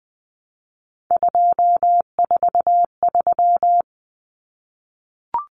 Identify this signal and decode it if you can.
Morse code, heard as a keyed tone, about 20 wpm, 700 Hz. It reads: 243
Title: Common Cut Numbers - 3-Digit 20wpm